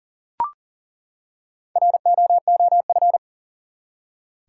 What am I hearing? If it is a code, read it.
Morse code, heard as a keyed tone, about 38 wpm, 700 Hz. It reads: ROOF
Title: Four Letter Words - 40wpm